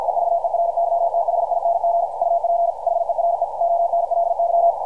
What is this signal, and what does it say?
Morse code, heard as a keyed tone, about 50 wpm, 680 Hz, with radio noise. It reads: EVERYTHING UP HERE ONLY
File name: archived / CRSnet